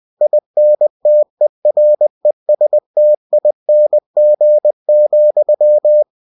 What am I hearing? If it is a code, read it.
Morse code, heard as a keyed tone, about 20 wpm, 600 Hz. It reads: INTERESTING,